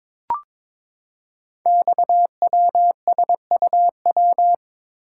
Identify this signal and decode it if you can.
Morse code, heard as a keyed tone, about 22 wpm, 700 Hz. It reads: XWSUW
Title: Code Groups - 5 Letters - 22wpm